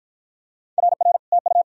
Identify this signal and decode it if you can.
Morse code, heard as a keyed tone, about 54 wpm, 700 Hz. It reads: RR TU